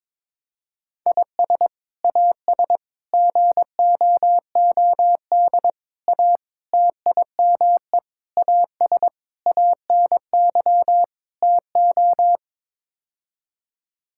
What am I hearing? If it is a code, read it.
Morse code, heard as a keyed tone, about 22 wpm, 700 Hz. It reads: IS AS GOOD A TIME AS ANY TO
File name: practice / 22wpm / Sets of 8 Words - 22wpm